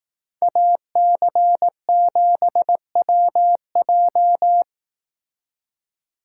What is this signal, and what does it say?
Morse code, heard as a keyed tone, about 18 wpm, 700 Hz. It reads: AC7WJ
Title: Speed Racing - USA 2x2 Call Signs 15wpm